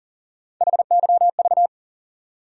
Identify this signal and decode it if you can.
Morse code, heard as a keyed tone, about 40 wpm, 700 Hz. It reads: HYV
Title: USA Call Sign Postfixes - 3 Letters 40wpm